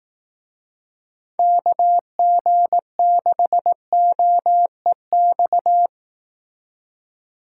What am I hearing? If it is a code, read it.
Morse code, heard as a keyed tone, about 18 wpm, 700 Hz. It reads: KG6OEX
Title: Speed Racing - USA 2x3 Call Signs 15wpm